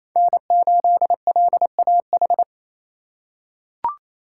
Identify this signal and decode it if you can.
Morse code, heard as a keyed tone, about 27 wpm, 700 Hz. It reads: N8LAH
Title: USA Call Signs - 5 Characters - 2x Character Spacing - 28wpm